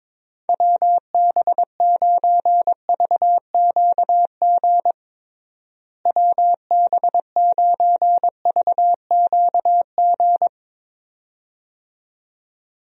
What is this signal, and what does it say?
Morse code, heard as a keyed tone, about 22 wpm, 700 Hz. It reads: WB9VQG WB9VQG
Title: USA Call Signs - 6 Characters - Repeated 2x - 22wpm